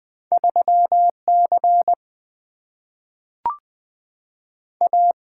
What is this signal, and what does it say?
Morse code, heard as a keyed tone, about 20 wpm, 700 Hz. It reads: 3C A
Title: Single-Letter Single-Number 20wpm